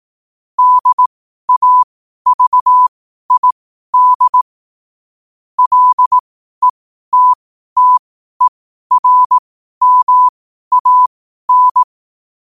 Morse code, heard as a keyed tone, about 18 wpm, spaced as for 12 wpm, 1000 Hz. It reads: DAVID LETTERMAN